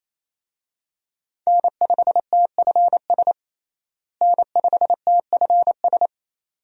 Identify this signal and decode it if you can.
Morse code, heard as a keyed tone, about 28 wpm, 700 Hz. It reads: N5TFS N5TFS
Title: USA Call Signs - 5 Characters - Repeated 2x - 28wpm